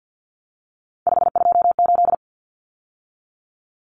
Morse code, heard as a keyed tone, about 48 wpm, 700 Hz. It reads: 528
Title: Numbers - 3-Digit 50wpm